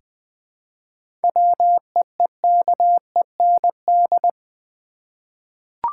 Morse code, heard as a keyed tone, about 20 wpm, 700 Hz. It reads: WEEKEND
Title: Seven Letter Words - 20wpm